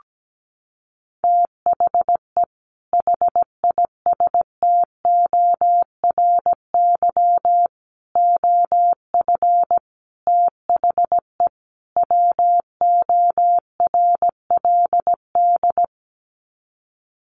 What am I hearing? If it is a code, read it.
Morse code, heard as a keyed tone, about 17 wpm, 700 Hz. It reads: THE HISTORY OF THE WORLD
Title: Sets of 5 Words Encore2 17wpm